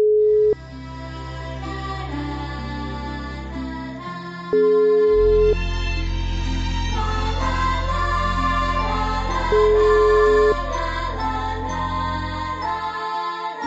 Ringback tone (Europe, 425 Hz), with noise.
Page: Uvítací melodii uslyší volající, než přijmete hovor.